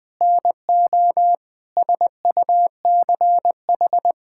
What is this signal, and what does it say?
Morse code, heard as a keyed tone, about 20 wpm, 700 Hz. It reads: NO SUCH
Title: Sentences with 8-Word Phrases - Spoken Beginning - 20wpm